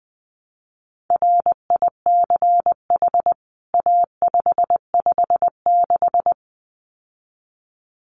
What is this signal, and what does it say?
Morse code, heard as a keyed tone, about 20 wpm, 700 Hz. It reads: RICH A556
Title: CWT Exchange 20wpm